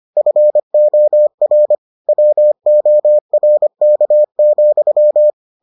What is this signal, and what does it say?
Morse code, heard as a keyed tone, about 25 wpm, 600 Hz. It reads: FOR WORK,